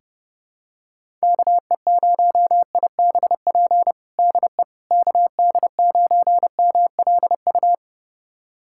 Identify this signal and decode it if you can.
Morse code, heard as a keyed tone, about 32 wpm, 700 Hz. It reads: KE0IBP DE KD9MLU
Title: USA Callsign DE Callsign - 6 Characters - 30wpm